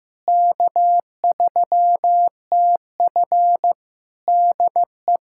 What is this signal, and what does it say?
Morse code, heard as a keyed tone, about 15 wpm, 700 Hz. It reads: K3TF DE
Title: Practice QSO - Exchange 4 - Ending - 15wpm